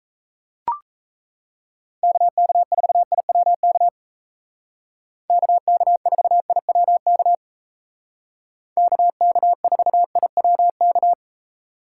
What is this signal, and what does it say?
Morse code, heard as a keyed tone, about 34 wpm, 700 Hz. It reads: KK4IWK KK4IWK KK4IWK
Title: Speed Racing - USA 2x3 Call Signs 28wpm